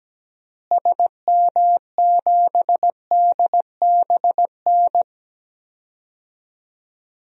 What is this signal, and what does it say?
Morse code, heard as a keyed tone, about 17 wpm, 700 Hz. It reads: SM7DBN
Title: DXCC-100 Call Signs - 6 Characters - Repeated 3x - 17wpm